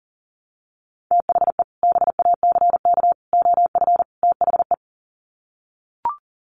Morse code, heard as a keyed tone, about 40 wpm, 700 Hz. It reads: THE BACK OF THE
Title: Sets of 4 Words 40wpm